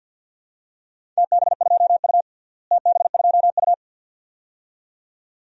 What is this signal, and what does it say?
Morse code, heard as a keyed tone, about 50 wpm, 700 Hz. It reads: TB2U TB2U
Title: DXCC-100 Call Signs - 4 Characters - Repeated 2x - 50wpm